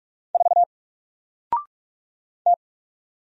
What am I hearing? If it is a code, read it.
Morse code, heard as a keyed tone, about 44 wpm, 700 Hz. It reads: 4 T
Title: Single Character (Letter:Number:Punctuation:Prosign) 45wpm